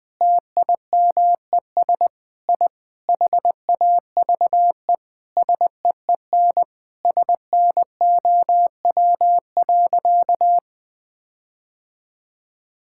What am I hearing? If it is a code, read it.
Morse code, heard as a keyed tone, about 21 wpm, 700 Hz. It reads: TIMES I HAVE SEEN SNOW.
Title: Sentences with 8-Word Phrases - 20wpm